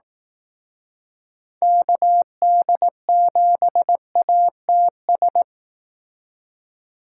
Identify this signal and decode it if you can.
Morse code, heard as a keyed tone, about 18 wpm, 700 Hz. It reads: KD7ATS
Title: Speed Racing - USA 2x3 Call Signs 15wpm